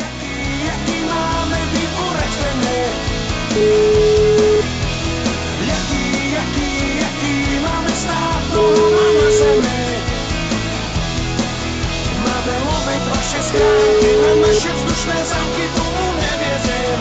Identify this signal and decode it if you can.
Ringback tone (Europe, 425 Hz), with noise.